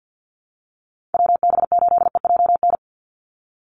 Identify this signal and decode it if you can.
Morse code, heard as a keyed tone, about 48 wpm, 700 Hz. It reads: WB8EJN
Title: Call Signs Repeated 3x - 50wpm